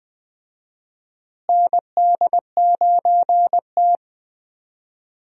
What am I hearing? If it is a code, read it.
Morse code, heard as a keyed tone, about 20 wpm, 700 Hz. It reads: ND9T
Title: USA Call Signs - 4 Characters - Repeated 2x - 20wpm